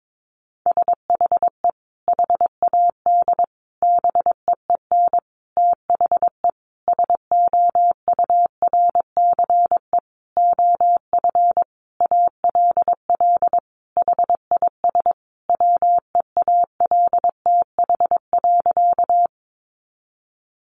Morse code, heard as a keyed tone, about 22 wpm, 700 Hz. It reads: SHE HAD BEEN THE SOURCE OF ALL HIS WEALTH.
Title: Sentences from Top 1200 Words 22wpm